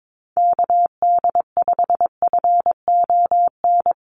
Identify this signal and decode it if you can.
Morse code, heard as a keyed tone, about 22 wpm, 700 Hz. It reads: KD5FON
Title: USA Call Signs Repeated 3x 22wpm